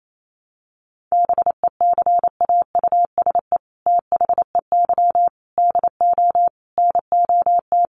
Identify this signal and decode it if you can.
Morse code, heard as a keyed tone, about 28 wpm, 700 Hz. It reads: BECAUSE THEY DO NOT
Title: Sets of 5 Words Encore2 28wpm